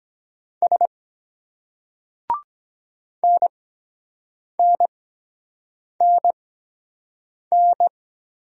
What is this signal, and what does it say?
Morse code, heard as a keyed tone, about 22 wpm, spaced as for 7 wpm, 700 Hz. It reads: S NNNN